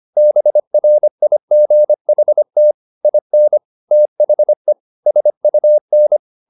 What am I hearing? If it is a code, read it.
Morse code, heard as a keyed tone, about 25 wpm, 600 Hz. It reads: BRIGHT IN THE SUN
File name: the old man and the sea part 1 25wpm